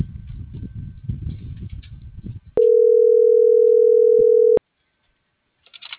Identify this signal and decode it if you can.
Ringback tone (North America), 440 + 480 Hz.